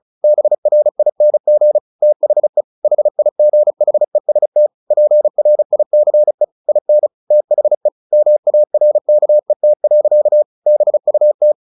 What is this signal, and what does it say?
Morse code, heard as a keyed tone, about 34 wpm, 600 Hz. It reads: BRING THE HIGHEST PRICE IN THE MARKET. BUT
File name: the old man and the sea 2 35wpm